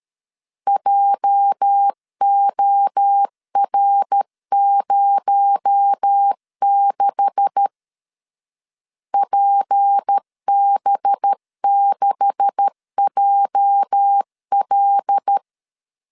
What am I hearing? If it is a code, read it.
Morse code, heard as a keyed tone, about 13 wpm, 800 Hz. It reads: JOR06 PB6JL
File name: LEZIONI / 9 286 12wpm 2305